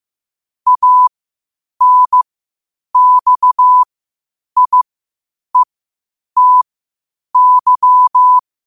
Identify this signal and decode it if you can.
Morse code, heard as a keyed tone, about 15 wpm, spaced as for 8 wpm, 1000 Hz. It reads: ANXIETY
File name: qotdfiles / 8wpm